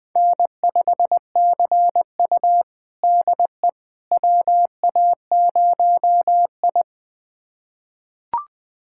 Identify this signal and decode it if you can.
Morse code, heard as a keyed tone, about 20 wpm, 700 Hz. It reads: N5CU DE WA0I
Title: USA Callsign DE Callsign - 4 Characters - 20wpm